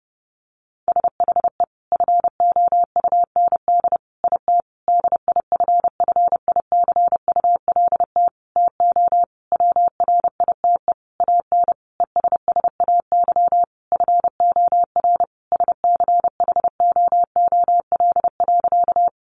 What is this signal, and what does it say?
Morse code, heard as a keyed tone, about 30 wpm, 700 Hz. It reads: SHE FOUND IT DIFFICULT TO WRITE AN ESSAY FOR SCHOOL.
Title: Sentences from Top 1500 Words 30wpm